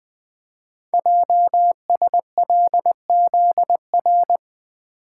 Morse code, heard as a keyed tone, about 20 wpm, 700 Hz. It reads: JSLZR
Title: Code Groups - 5 Letters - 20wpm